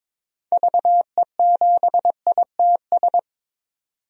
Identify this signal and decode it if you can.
Morse code, heard as a keyed tone, about 22 wpm, 700 Hz. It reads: VE7ITS
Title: DXCC-100 Call Signs - 6 Characters - 22wpm